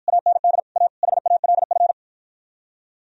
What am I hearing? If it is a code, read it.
Morse code, heard as a keyed tone, about 51 wpm, 700 Hz. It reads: AND A HALF